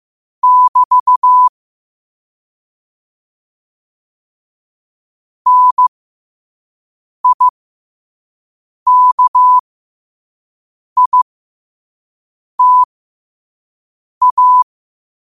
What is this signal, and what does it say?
Morse code, heard as a keyed tone, about 15 wpm, spaced as for 5 wpm, 1000 Hz. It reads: <BT> NIKITA
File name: qotdfiles / 5wpm